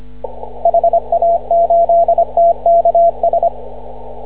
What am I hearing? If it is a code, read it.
Morse code, heard as a keyed tone, about 25 wpm, 680 Hz. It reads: HA8TKS
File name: CW / ha6oi 2